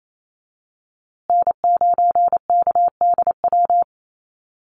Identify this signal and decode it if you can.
Morse code, heard as a keyed tone, about 28 wpm, 700 Hz. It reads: N9KDW
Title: USA Call Signs Repeated 3x 28wpm